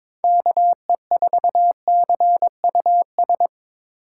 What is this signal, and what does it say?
Morse code, heard as a keyed tone, about 22 wpm, 700 Hz. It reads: KE4CUS
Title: USA Call Signs - 6 Characters - 6x Character Spacing - 22wpm